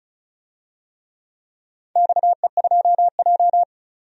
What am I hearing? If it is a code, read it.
Morse code, heard as a keyed tone, about 36 wpm, 690 Hz. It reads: XE2J
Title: DXCC-100 Call Signs - 4 Characters - 35wpm